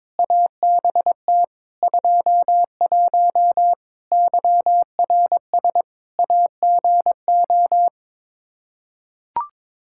Morse code, heard as a keyed tone, about 22 wpm, 700 Hz. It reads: ABT 21 YRS AGO
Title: Practice QSO - Exchange 3 - More Info - 22wpm